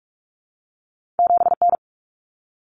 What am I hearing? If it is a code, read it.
Morse code, heard as a keyed tone, about 46 wpm, 700 Hz. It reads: 7N
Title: Common Cut Numbers - 2-Digit 45wpm